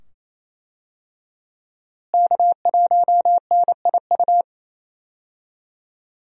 Morse code, heard as a keyed tone, about 28 wpm, 700 Hz. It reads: K1NIU
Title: USA Call Signs - 5 Characters - Repeated 2x - 28wpm